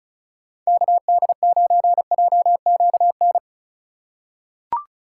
Morse code, heard as a keyed tone, about 36 wpm, 700 Hz. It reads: KD9JQN
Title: USA Call Signs - 6 Characters - 3x Character Spacing - 35wpm